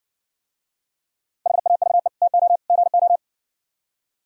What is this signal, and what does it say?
Morse code, heard as a keyed tone, about 56 wpm, 700 Hz. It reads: HAVE TO GO